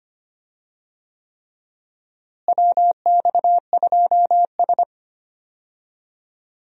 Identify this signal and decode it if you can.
Morse code, heard as a keyed tone, about 25 wpm, 700 Hz. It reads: WX2S
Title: USA Call Signs - 4 Characters - Repeated 3x - 25wpm